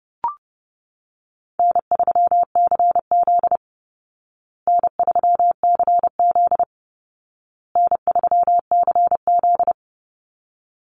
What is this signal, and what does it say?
Morse code, heard as a keyed tone, about 30 wpm, 700 Hz. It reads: N3CZ N3CZ N3CZ
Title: Easy Call Signs Repeated 3x - 30wpm